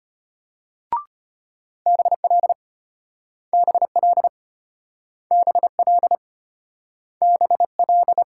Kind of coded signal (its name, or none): Morse code